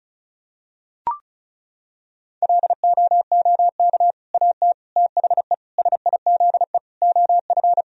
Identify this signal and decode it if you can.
Morse code, heard as a keyed tone, about 34 wpm, 700 Hz. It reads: LOOK AT THE SIZE OF
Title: Sentences from Top 1000 Words - Review Part 2 - 35wpm